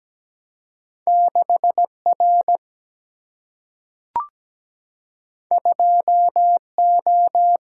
Morse code, heard as a keyed tone, about 17 wpm, 700 Hz. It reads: 6 R 2 O
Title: Single-Letter Single-Number 17wpm